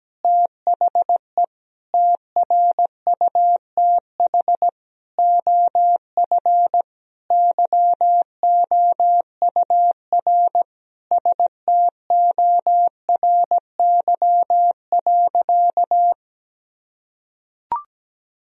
Morse code, heard as a keyed tone, about 17 wpm, 700 Hz. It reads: THE TRUTH OF YOUR STORY.
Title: Sentences from Top 1000 Words - Review Part 3 - 17wpm